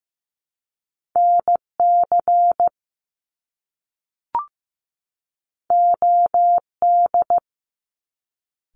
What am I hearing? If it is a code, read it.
Morse code, heard as a keyed tone, about 15 wpm, 700 Hz. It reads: NC OD
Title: Top Two Letter Combinations in Words 15wpm